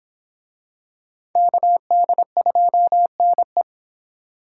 Morse code, heard as a keyed tone, about 26 wpm, 700 Hz. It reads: KD2NE